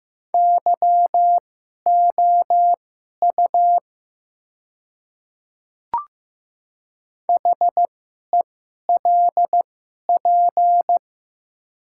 Morse code, heard as a keyed tone, about 15 wpm, spaced as for 11 wpm, 700 Hz. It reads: YOU HELP
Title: 2x Character Spacing - Top 200 Words - 15wpm